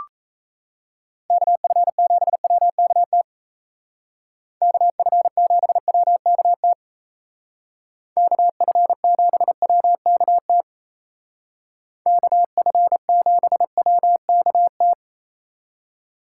Morse code, heard as a keyed tone, about 36 wpm, 700 Hz. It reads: KF7WKT KF7WKT KF7WKT KF7WKT